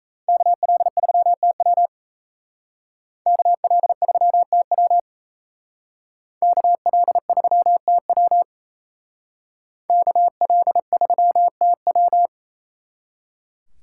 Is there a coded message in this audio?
Yes: Morse code from 0.3 to 12.3 s, heard as a keyed tone, about 35 wpm, 700 Hz. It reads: KL3TW KL3TW KL3TW KL3TW